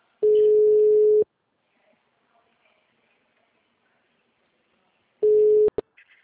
Ringback tone (Europe, 425 Hz).
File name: out- -2010- - - .226